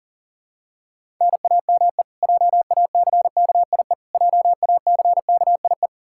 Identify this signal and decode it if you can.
Morse code, heard as a keyed tone, about 40 wpm, 700 Hz. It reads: NAME JACKIE JACKIE
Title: Practice QSO - Exchange 1 - Beginning - 40wpm